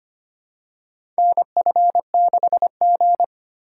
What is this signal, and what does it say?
Morse code, heard as a keyed tone, about 25 wpm, 700 Hz. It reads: NF6G
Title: USA Call Signs - 4 Characters - 2x Character Spacing - 25wpm